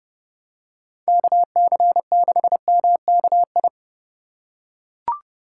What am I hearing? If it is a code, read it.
Morse code, heard as a keyed tone, about 30 wpm, 700 Hz. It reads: KC6MKI
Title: USA Call Signs - 6 Characters - 3x Character Spacing - 30wpm